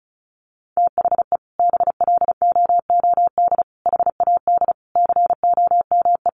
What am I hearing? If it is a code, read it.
Morse code, heard as a keyed tone, about 34 wpm, 700 Hz. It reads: THE BLOOD HAD COME
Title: Sentences from Top 1300 Words 35wpm